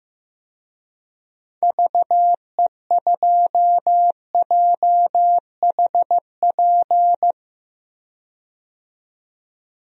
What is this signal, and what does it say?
Morse code, heard as a keyed tone, about 15 wpm, 700 Hz. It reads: VE2JHP
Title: DXCC-100 Call Signs - 6 Characters - 15wpm